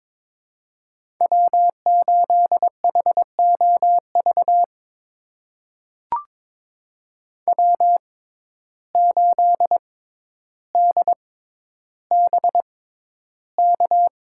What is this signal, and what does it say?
Morse code, heard as a keyed tone, about 22 wpm, spaced as for 11 wpm, 700 Hz. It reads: W8HOV W8DBK
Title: USA Call Signs - 6x Character Spacing - 22wpm